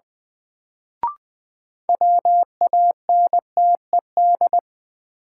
Morse code, heard as a keyed tone, about 20 wpm, 700 Hz. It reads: WANTED
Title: Sets of 2 Words - New Word Drill - 20wpm